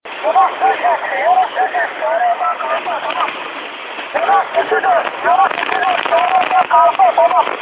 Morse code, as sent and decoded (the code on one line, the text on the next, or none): none